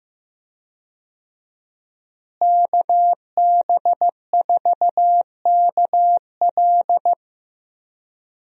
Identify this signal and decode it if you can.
Morse code, heard as a keyed tone, about 15 wpm, 700 Hz. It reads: KB4KL
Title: USA Call Signs - 5 Characters - Repeated 3x - 15wpm